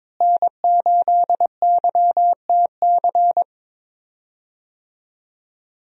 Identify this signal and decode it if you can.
Morse code, heard as a keyed tone, about 22 wpm, 700 Hz. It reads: N8YTC
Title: USA Call Signs - 5 Characters - Repeated 2x - 22wpm